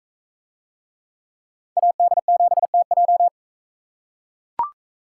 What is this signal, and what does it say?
Morse code, heard as a keyed tone, about 39 wpm, 700 Hz. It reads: AD7TJ